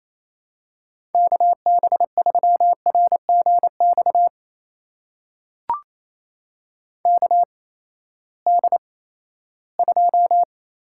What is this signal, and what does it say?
Morse code, heard as a keyed tone, about 28 wpm, spaced as for 16 wpm, 700 Hz. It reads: KB3RGX KD2
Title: USA Call Signs - 6 Characters - 8x Character Spacing - 28wpm